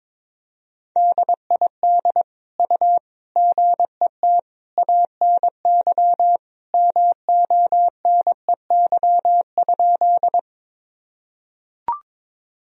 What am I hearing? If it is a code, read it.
Morse code, heard as a keyed tone, about 22 wpm, 700 Hz. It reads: DID U GET ANY MONEY?
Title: Sentences from Top 400 Words and Abbreviations 22wpm